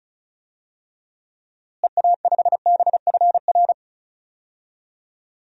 Morse code, heard as a keyed tone, about 34 wpm, 700 Hz. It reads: EA5BFR
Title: DXCC-100 Call Signs - 6 Characters - Repeated 3x - 35wpm